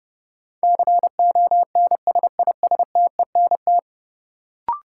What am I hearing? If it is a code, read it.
Morse code, heard as a keyed tone, about 30 wpm, 700 Hz. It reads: CONSISTENT
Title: Sentences from Top 1500 Words - New Word Drill Alt - 30wpm